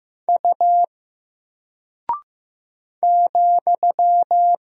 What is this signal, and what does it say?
Morse code, heard as a keyed tone, about 15 wpm, 700 Hz. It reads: U ,
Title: Single Character (Letter:Number:Punctuation:Prosign) 15wpm